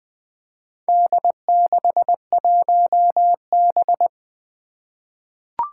Morse code, heard as a keyed tone, about 20 wpm, 700 Hz. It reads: D61B
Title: Four Challenging Letter-Number Combinations 20wpm